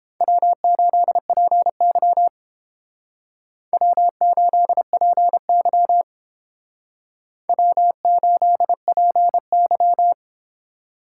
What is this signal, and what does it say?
Morse code, heard as a keyed tone, about 32 wpm, 700 Hz. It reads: W8PY W8PY W8PY